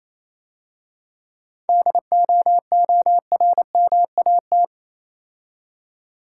Morse code, heard as a keyed tone, about 28 wpm, 690 Hz. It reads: DOORMAT